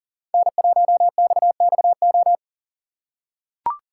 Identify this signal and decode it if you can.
Morse code, heard as a keyed tone, about 40 wpm, 700 Hz. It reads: N1XXO
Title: USA Call Signs - 5 Characters - 2x Character Spacing - 40wpm